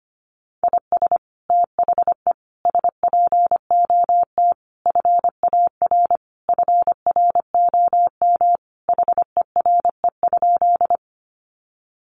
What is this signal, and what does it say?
Morse code, heard as a keyed tone, about 25 wpm, 700 Hz. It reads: IS THE SPOT FAR FROM HERE?
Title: Sentences from Top 1100 Words 25wpm